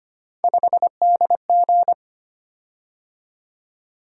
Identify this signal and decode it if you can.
Morse code, heard as a keyed tone, about 25 wpm, 700 Hz. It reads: 5DG
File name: Three Letter-Number Combinations - 25wpm